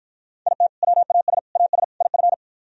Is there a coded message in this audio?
Yes: Morse code from 0.5 to 2.3 s, heard as a keyed tone, about 52 wpm, 700 Hz. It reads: IT WAS AS IF